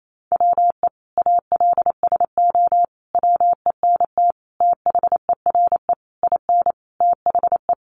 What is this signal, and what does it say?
Morse code, heard as a keyed tone, about 28 wpm, 700 Hz. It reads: WE ALSO WENT THERE IN THE
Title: Sentences from Top 1200 Words 28wpm